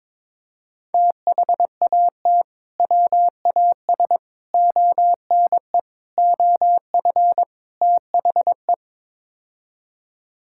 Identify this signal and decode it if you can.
Morse code, heard as a keyed tone, about 21 wpm, 700 Hz. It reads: THAT WAS ONE OF THE
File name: practice / 22wpm / Build A Phrase - 5-Word Phrases - 22wpm